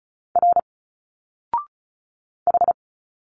Morse code, heard as a keyed tone, about 36 wpm, 700 Hz. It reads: R H
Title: Single Letters 35wpm